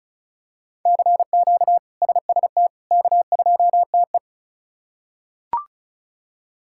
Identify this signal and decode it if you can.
Morse code, heard as a keyed tone, about 34 wpm, 700 Hz. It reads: CQ SST K2TE
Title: K1USN SST QSO Practice - 35wpm